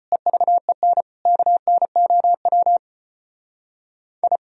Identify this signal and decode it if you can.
Morse code, heard as a keyed tone, about 33 wpm, 700 Hz. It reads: EVEN KNOW I